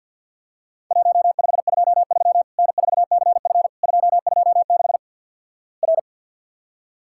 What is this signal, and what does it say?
Morse code, heard as a keyed tone, about 50 wpm, 700 Hz. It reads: 1523 N4CF 226 R
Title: Worked All Europe DX Contest - QTC Exchange with Repeats - 50wpm